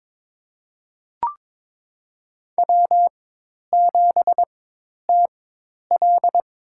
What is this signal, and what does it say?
Morse code, heard as a keyed tone, about 22 wpm, spaced as for 10 wpm, 700 Hz. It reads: W7TL
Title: USA Call Signs - 5 Characters - 4x Character Spacing - 22wpm